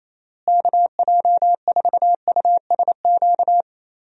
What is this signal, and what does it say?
Morse code, heard as a keyed tone, about 27 wpm, 700 Hz. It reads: KJ4USQ
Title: USA Call Signs - 6 Characters - Repeated 2x - 28wpm